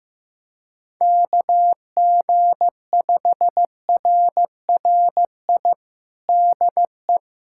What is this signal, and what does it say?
Morse code, heard as a keyed tone, about 15 wpm, 700 Hz. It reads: KG5RRI DE
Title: USA Callsign DE Callsign 15wpm